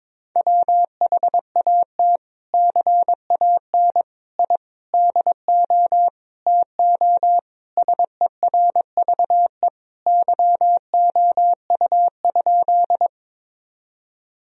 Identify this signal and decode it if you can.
Morse code, heard as a keyed tone, about 22 wpm, 700 Hz. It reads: WHAT CAN I DO TO SERVE YOU?
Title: Sentences from Top 1000 Words - Review Part 2 - 22wpm